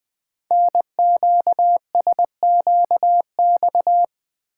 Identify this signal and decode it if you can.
Morse code, heard as a keyed tone, about 20 wpm, 700 Hz. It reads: NQSQX
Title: Code Groups - 5 Letters - 20wpm